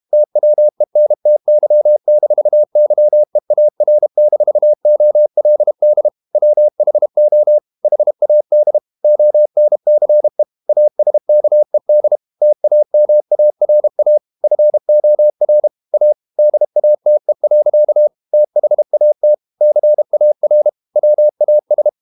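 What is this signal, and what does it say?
Morse code, heard as a keyed tone, about 32 wpm, 600 Hz. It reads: TWENTY-YEAR-OLD WHO HAD ONCE ASKED TAMARA FOR A DATE. THAT CAR WAS
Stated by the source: Morse code practice at 32 wpm from CW med Gnister, reading Ken Follett's novel Never in English.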